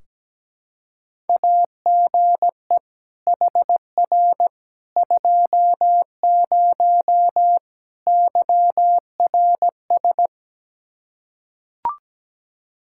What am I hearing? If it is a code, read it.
Morse code, heard as a keyed tone, about 17 wpm, 700 Hz. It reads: AGE HR 20 YRS
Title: Practice QSO - Exchange 3 - More Info - 17wpm